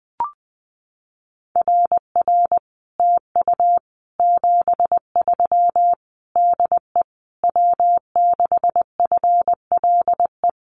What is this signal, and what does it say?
Morse code, heard as a keyed tone, about 19 wpm, 700 Hz. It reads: RR TU 73 DE W6FLE
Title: Common QSO Phrases Encore 20wpm